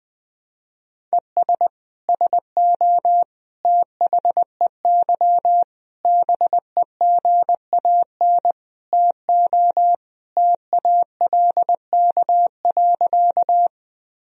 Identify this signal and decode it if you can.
Morse code, heard as a keyed tone, about 21 wpm, 700 Hz. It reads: ES SO THEY BEGAN TO TALK.
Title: Sentences from Top 400 Words and Abbreviations 20wpm